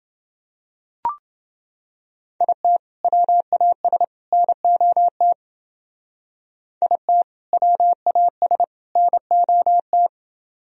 Morse code, heard as a keyed tone, about 29 wpm, 700 Hz. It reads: IT WAS NOT IT WAS NOT